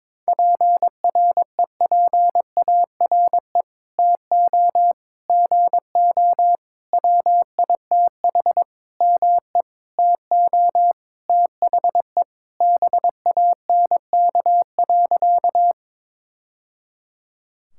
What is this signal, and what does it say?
Morse code, heard as a keyed tone, about 22 wpm, 700 Hz. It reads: PREPARE TO GO WITH ME TO THE BANK.
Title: Sentences from Top 1000 Words - Review Part 6 - 22wpm